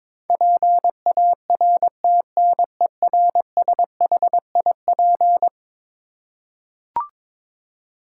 Morse code, heard as a keyed tone, about 22 wpm, 700 Hz. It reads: PARTNERSHIP
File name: practice / 22wpm / Eleven Letter Words - 22wpm